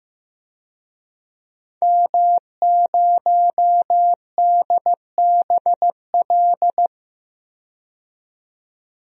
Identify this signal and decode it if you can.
Morse code, heard as a keyed tone, about 15 wpm, 700 Hz. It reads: M0DBL
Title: DXCC-100 Call Signs - 5 Characters - 15wpm